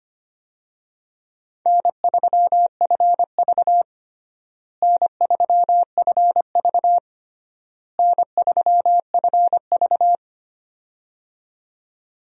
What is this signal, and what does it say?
Morse code, heard as a keyed tone, about 25 wpm, 700 Hz. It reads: N3FV N3FV N3FV
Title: USA Call Signs - 4 Characters - Repeated 3x - 25wpm